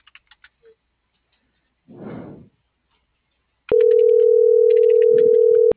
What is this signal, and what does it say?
Ringback tone (North America), 440 + 480 Hz.